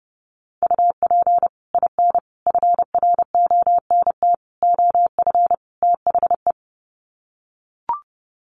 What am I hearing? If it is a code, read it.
Morse code, heard as a keyed tone, about 30 wpm, 700 Hz. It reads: UP IN FRONT OF THE
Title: Sets of 5 Words Encore2 30wpm